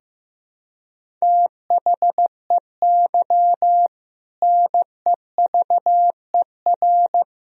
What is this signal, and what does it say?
Morse code, heard as a keyed tone, about 15 wpm, 700 Hz. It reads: THEY NEVER
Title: Sentences from Top 300 Words and Abbreviations 15wpm